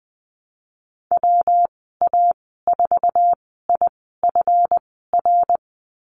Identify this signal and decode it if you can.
Morse code, heard as a keyed tone, about 20 wpm, spaced as for 15 wpm, 700 Hz. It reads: WA4IFR
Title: 186 Lesson 30g - Letter b - Call Signs 20wpm